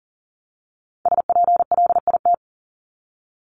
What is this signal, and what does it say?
Morse code, heard as a keyed tone, about 40 wpm, 700 Hz. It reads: SPLIT
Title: Top 1751-2000 Words 40wpm